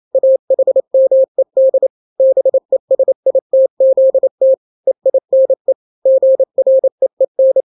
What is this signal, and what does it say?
Morse code, heard as a keyed tone, about 28 wpm, 530 Hz. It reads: AHMED BESITZT EINE GREEN
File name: daily / 27wpm